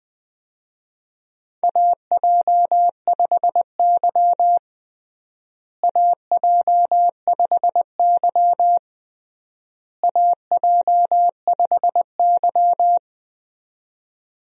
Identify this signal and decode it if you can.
Morse code, heard as a keyed tone, about 20 wpm, 700 Hz. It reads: AJ5Y AJ5Y AJ5Y
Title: USA Call Signs - 4 Characters - Repeated 3x - 20wpm